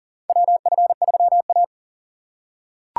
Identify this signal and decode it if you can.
Morse code, heard as a keyed tone, about 40 wpm, 700 Hz. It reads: WF3A
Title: USA Call Signs - 4 Characters - 3x Character Spacing - 40wpm